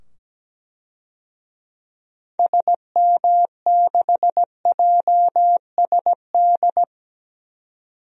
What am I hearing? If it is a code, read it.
Morse code, heard as a keyed tone, about 17 wpm, 700 Hz. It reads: SM6JSD
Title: DXCC-100 Call Signs - 6 Characters - 17wpm